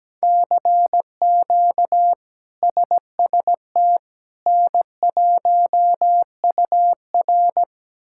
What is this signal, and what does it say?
Morse code, heard as a keyed tone, about 17 wpm, 700 Hz. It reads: CQ SST N1UR
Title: K1USN SST QSO Practice - 17wpm